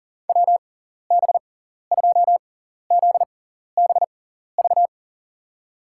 Morse code, heard as a keyed tone, about 37 wpm, spaced as for 14 wpm, 700 Hz. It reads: WB2ZBV
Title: USA Call Signs - 6x Character Spacing - 40wpm